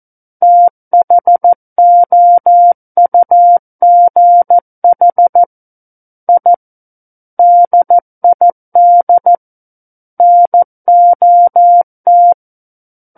Morse code, heard as a keyed tone, about 14 wpm, 700 Hz. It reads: THOUGH I DID NOT
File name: Book One / 05-Chapter 5 - 14 WPM